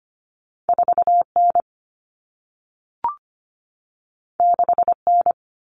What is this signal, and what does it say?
Morse code, heard as a keyed tone, about 25 wpm, 700 Hz. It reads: 4N 6N
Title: Common Cut Numbers - 2-Digit 25wpm